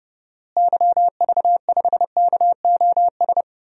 Morse code, heard as a keyed tone, about 31 wpm, 700 Hz. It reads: YV5KOS
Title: DXCC-100 Call Signs - 6 Characters - Repeated 2x - 30wpm